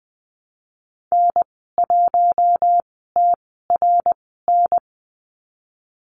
Morse code, heard as a keyed tone, about 20 wpm, spaced as for 15 wpm, 700 Hz. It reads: N1TRN
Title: 055 Lesson 10g - Letter r - Call Signs 20wpm